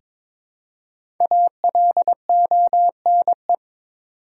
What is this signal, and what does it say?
Morse code, heard as a keyed tone, about 22 wpm, 700 Hz. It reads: ALONE
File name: 5 Letter Words - First 3 Letters Given - 22wpm